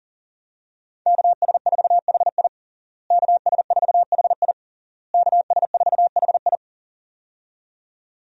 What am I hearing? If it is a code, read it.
Morse code, heard as a keyed tone, about 40 wpm, 700 Hz. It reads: KS4HI KS4HI KS4HI
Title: USA Call Signs - 5 Characters - Repeated 3x - 40wpm